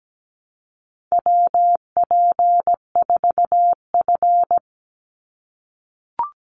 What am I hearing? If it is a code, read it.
Morse code, heard as a keyed tone, about 17 wpm, 700 Hz. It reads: WP4F
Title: Call Signs - 17wpm